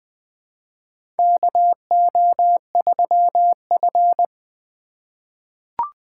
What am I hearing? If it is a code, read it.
Morse code, heard as a keyed tone, about 20 wpm, 700 Hz. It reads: KO3F
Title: USA Call Signs - 4 Characters - 4x Character Spacing - 20wpm